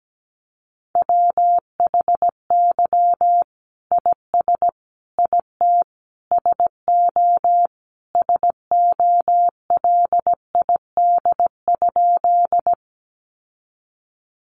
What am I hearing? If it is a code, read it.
Morse code, heard as a keyed tone, about 17 wpm, 700 Hz. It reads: WHY IS IT SO SOLID?
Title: Sentences from Top 1500 Words 17wpm